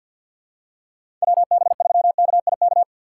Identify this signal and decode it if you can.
Morse code, heard as a keyed tone, about 53 wpm, 700 Hz. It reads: WB3KIK
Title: USA Call Signs - 2x Character Spacing - 50wpm